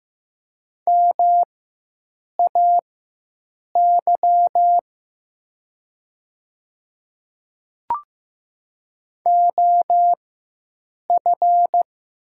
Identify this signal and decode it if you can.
Morse code, heard as a keyed tone, about 15 wpm, spaced as for 7 wpm, 700 Hz. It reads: MAY OF
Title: 4x Character Spacing - Top 100 Words - 15wpm